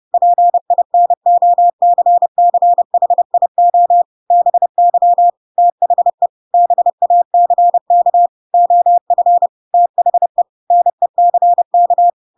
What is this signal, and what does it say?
Morse code, heard as a keyed tone, about 30 wpm, 700 Hz. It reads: PINOCCHIO BY THE BACK OF THE NECK
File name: Pinocchio / Pinocchio ch3 30 30 ik1hge